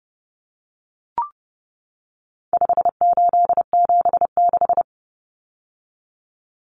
Morse code, heard as a keyed tone, about 30 wpm, 700 Hz. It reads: 5876
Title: Numbers - 4-Digit 30wpm